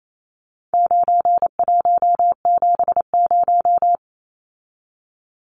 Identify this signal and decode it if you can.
Morse code, heard as a keyed tone, about 28 wpm, 700 Hz. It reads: 9170
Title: Numbers - 4-Digit 28wpm